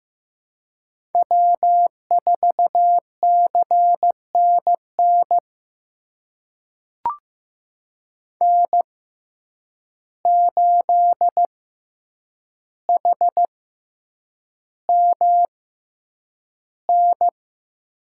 Morse code, heard as a keyed tone, about 15 wpm, spaced as for 8 wpm, 700 Hz. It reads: W4CNN N8HMN
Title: USA Call Signs - 5 Characters - 6x Character Spacing - 15wpm